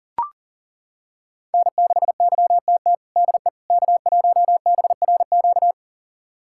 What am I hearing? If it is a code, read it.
Morse code, heard as a keyed tone, about 40 wpm, 700 Hz. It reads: N6YTT DE K1BRQ
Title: USA Callsign DE Callsign 40wpm